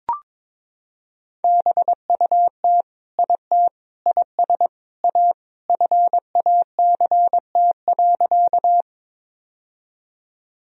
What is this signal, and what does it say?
Morse code, heard as a keyed tone, about 22 wpm, 700 Hz. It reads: BUT IT IS A FACT.
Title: Sentences from Top 1000 Words - Review Part 3 - 22wpm